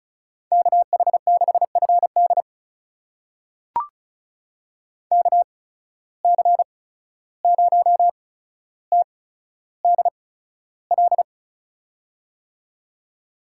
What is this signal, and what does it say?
Morse code, heard as a keyed tone, about 36 wpm, spaced as for 14 wpm, 700 Hz. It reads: KH6FD KC0TDL
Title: USA Call Signs - 8x Character Spacing - 35wpm